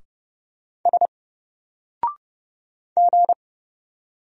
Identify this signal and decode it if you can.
Morse code, heard as a keyed tone, about 30 wpm, 700 Hz. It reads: S G
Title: Single Character (Letter:Number:Punctuation:Prosign) Alt 30wpm